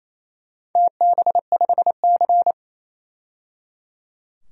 Morse code, heard as a keyed tone, about 29 wpm, 690 Hz. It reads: TB5C
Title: DXCC-100 Call Signs - 4 Characters - 28wpm